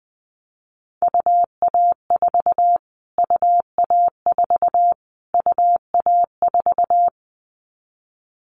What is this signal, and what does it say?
Morse code, heard as a keyed tone, about 20 wpm, 700 Hz. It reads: UA4 UA4 UA4
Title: Callsign Prefixes for Top 25 DXCC - Repeated 3x - 20wpm